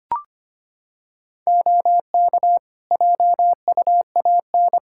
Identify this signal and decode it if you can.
Morse code, heard as a keyed tone, about 25 wpm, 700 Hz. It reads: OK JUAN
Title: Practice QSO - Exchange 3 - More Info - 25wpm